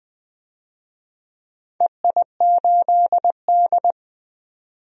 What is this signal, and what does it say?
Morse code, heard as a keyed tone, about 20 wpm, 700 Hz. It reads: EI8D
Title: DXCC-100 Call Signs - 4 Characters - 20wpm